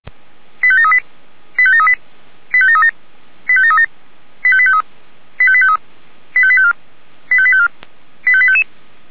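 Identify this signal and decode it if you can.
Five-tone selective call (ZVEI), heard as quick runs of five tones, tones of about 70 ms. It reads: 85738 85738 85737 85737 85853 85853 85854 85854 85855
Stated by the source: ZVEI five-tone call 85723, received by SDR.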